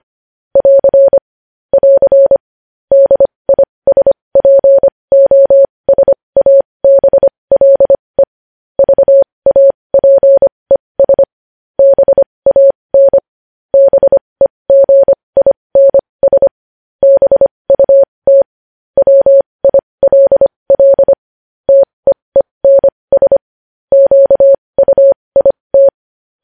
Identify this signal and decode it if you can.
Morse code, heard as a keyed tone, about 25 wpm, spaced as for 20 wpm, 550 Hz. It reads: <AR> <AR> DISPOSABLE VAPES BAN BEGINS BUT WILL TEENS QUIT
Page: news-headlines-in-morse-code-at-20-wpm